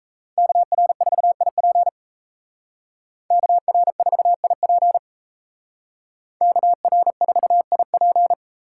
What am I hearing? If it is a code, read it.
Morse code, heard as a keyed tone, about 36 wpm, 700 Hz. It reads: KR4IP KR4IP KR4IP